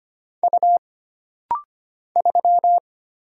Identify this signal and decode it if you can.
Morse code, heard as a keyed tone, about 25 wpm, 700 Hz. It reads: U 3
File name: Single Character (Letter:Number:Punctuation:Prosign) - Rapid-Fire - 25wpm